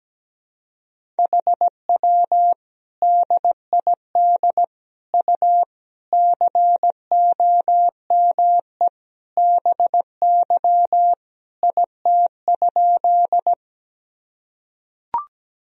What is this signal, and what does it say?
Morse code, heard as a keyed tone, about 17 wpm, 700 Hz. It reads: HW DID U COME BY IT?
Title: Sentences from Top 100 Words and Abbreviations 17wpm